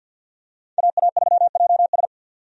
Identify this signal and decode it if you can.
Morse code, heard as a keyed tone, about 47 wpm, 700 Hz. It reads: AA3JS
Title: USA Call Signs - 5 Characters - 50wpm